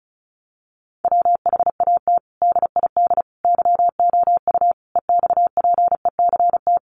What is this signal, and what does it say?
Morse code, heard as a keyed tone, about 34 wpm, 700 Hz. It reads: WHAT DID YOU EXPECT
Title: Sentences from Top 1100 Words 35wpm